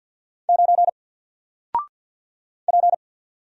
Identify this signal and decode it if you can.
Morse code, heard as a keyed tone, about 49 wpm, 700 Hz. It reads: 9 P
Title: Single Letter-Number Alt 50wpm